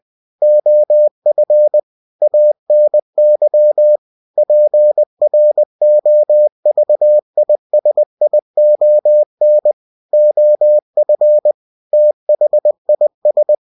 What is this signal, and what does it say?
Morse code, heard as a keyed tone, about 20 wpm, 600 Hz. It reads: OF ANY PROVISION OF THIS
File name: cw-books / winnie-the-pooh / winnie-20wpm-page-0119